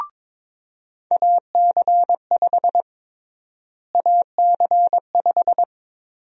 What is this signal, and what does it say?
Morse code, heard as a keyed tone, about 22 wpm, 700 Hz. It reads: AC5 AC5
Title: USA Call Sign Prefixes - Repeated 2x - 22wpm